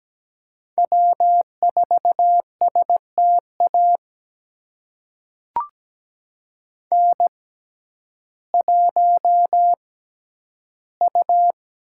Morse code, heard as a keyed tone, about 17 wpm, spaced as for 10 wpm, 700 Hz. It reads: W4STA N1U
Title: USA Call Signs - 5 Characters - 6x Character Spacing - 17wpm